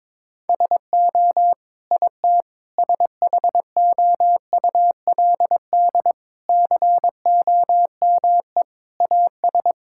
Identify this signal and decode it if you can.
Morse code, heard as a keyed tone, about 22 wpm, 700 Hz. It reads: SO IT SHOULD COME AS
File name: practice / 22wpm / Sets of 8 Words - 22wpm